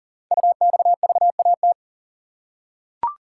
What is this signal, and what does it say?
Morse code, heard as a keyed tone, about 40 wpm, 690 Hz. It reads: UXVAT
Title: Code Groups - 5 Letters - 40wpm